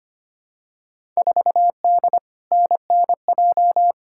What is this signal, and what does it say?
Morse code, heard as a keyed tone, about 25 wpm, 700 Hz. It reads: 4D NNJ
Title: Field Day Exchanges Encore - 25wpm